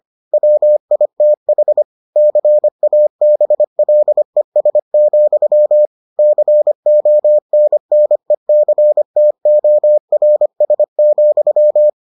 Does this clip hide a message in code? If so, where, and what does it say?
Yes: Morse code from 0.3 to 11.9 s, heard as a keyed tone, about 25 wpm, 600 Hz. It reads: WITH CABLES, CONNECTORS,